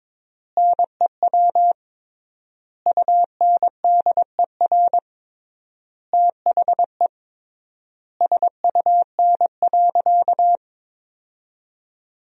Morse code, heard as a keyed tone, about 22 wpm, 700 Hz. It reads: NEW UNDER THE SUN.
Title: 3x Word-Spacing - Sentences from Top 500 Words - 22wpm